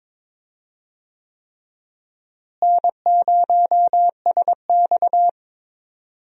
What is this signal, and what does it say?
Morse code, heard as a keyed tone, about 22 wpm, 700 Hz. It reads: N0SX
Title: USA Call Signs - 4 Characters - Repeated 3x - 22wpm